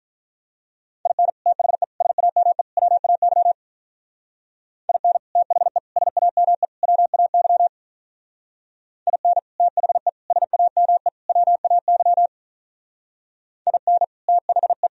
Morse code, heard as a keyed tone, about 43 wpm, 700 Hz. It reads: IN THE SAME WAY IN THE SAME WAY IN THE SAME WAY IN THE
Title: Speed-Racing - Sets of 4 Words 35wpm